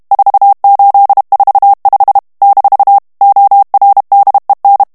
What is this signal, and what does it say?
Morse code, heard as a keyed tone, about 32 wpm, 800 Hz. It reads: 4845 - ORDEN
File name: morse mp3 / ESPAGNOL TEXTE EN CLAIR 2 VITESSE 30 MOTS MINUTES